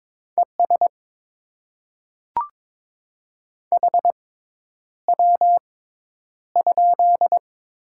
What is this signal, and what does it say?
Morse code, heard as a keyed tone, about 22 wpm, spaced as for 10 wpm, 690 Hz. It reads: ES HW?
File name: Top 20 QSO Elements - 6x Character Spacing - 22wpm